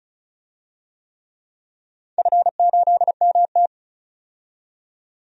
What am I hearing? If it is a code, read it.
Morse code, heard as a keyed tone, about 36 wpm, 700 Hz. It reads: F8MT
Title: DXCC-100 Call Signs - 4 Characters - 35wpm